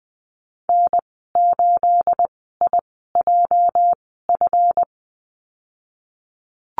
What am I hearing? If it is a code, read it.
Morse code, heard as a keyed tone, about 20 wpm, spaced as for 15 wpm, 700 Hz. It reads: N8IJF
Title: 231 Lesson 37g - Letter x - Call Signs 20wpm